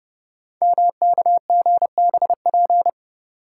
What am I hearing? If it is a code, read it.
Morse code, heard as a keyed tone, about 30 wpm, 700 Hz. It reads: MKGBP
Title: Code Groups - 5 Letters - 30wpm